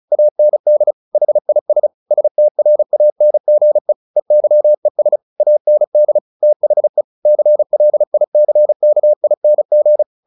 Morse code, heard as a keyed tone, about 34 wpm, 600 Hz. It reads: AND HIS STRANGE EYES AND THE CLICKING
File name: the old man and the sea 2 35wpm